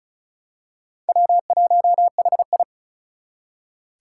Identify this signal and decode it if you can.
Morse code, heard as a keyed tone, about 36 wpm, 700 Hz. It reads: W1HI
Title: Four Challenging Letter-Number Combinations 35wpm